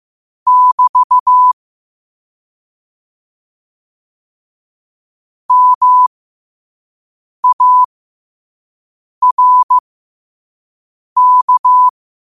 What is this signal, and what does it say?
Morse code, heard as a keyed tone, about 15 wpm, spaced as for 5 wpm, 1000 Hz. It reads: <BT> MARK
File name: qotdfiles / 5wpm